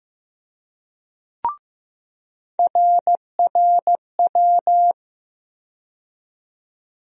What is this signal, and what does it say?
Morse code, heard as a keyed tone, about 15 wpm, 700 Hz. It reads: RRW
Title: USA Call Sign Postfixes - 3 Letters 15wpm